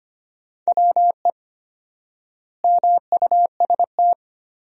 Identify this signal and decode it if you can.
Morse code, heard as a keyed tone, about 25 wpm, 700 Hz. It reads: WE MUST
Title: 4x Word-Spacing - Sentences from Top 200 Words - 25wpm